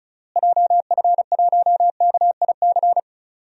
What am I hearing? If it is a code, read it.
Morse code, heard as a keyed tone, about 34 wpm, 700 Hz. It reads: JF1KIC
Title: DXCC-100 Call Signs - 6 Characters - Repeated 3x - 35wpm